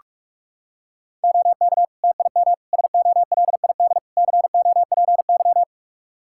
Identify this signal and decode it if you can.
Morse code, heard as a keyed tone, about 42 wpm, 700 Hz. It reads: OK TIM SOLID COPY
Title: Practice QSO - Exchange 3 - More Info - 45wpm